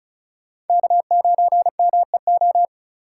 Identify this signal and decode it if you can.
Morse code, heard as a keyed tone, about 36 wpm, 700 Hz. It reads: K9MEO
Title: USA Call Signs - 5 Characters - 4x Character Spacing - 35wpm